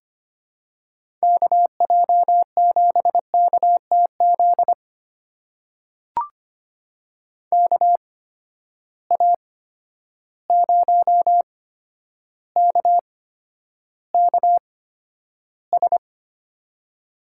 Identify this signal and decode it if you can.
Morse code, heard as a keyed tone, about 25 wpm, spaced as for 11 wpm, 700 Hz. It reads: KJ7KTZ KA0KKS
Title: USA Call Signs - 8x Character Spacing - 25wpm